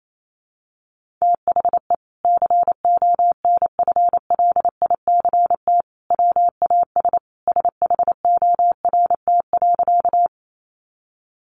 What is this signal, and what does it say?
Morse code, heard as a keyed tone, about 28 wpm, 700 Hz. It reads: THE CONFLICT WAS SHORT.
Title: Sentences from Top 1300 Words 28wpm